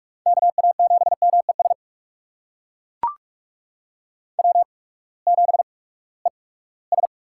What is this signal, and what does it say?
Morse code, heard as a keyed tone, about 42 wpm, spaced as for 22 wpm, 700 Hz. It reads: KA7MES W7ES
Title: USA Call Signs - 8x Character Spacing - 45wpm